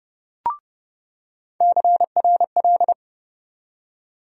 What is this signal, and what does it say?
Morse code, heard as a keyed tone, about 30 wpm, 700 Hz. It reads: CRL
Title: USA Call Sign Postfixes - 3 Letters 30wpm